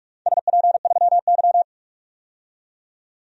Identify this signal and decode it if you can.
Morse code, heard as a keyed tone, about 45 wpm, 700 Hz. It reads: SP3Y
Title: DXCC-100 Call Signs - 4 Characters - 45wpm